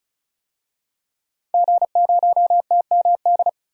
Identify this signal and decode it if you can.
Morse code, heard as a keyed tone, about 36 wpm, 700 Hz. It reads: G0TMD
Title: DXCC-100 Call Signs - 5 Characters - 35wpm